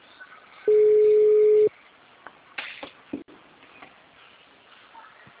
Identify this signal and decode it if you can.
Ringback tone (Europe, 425 Hz).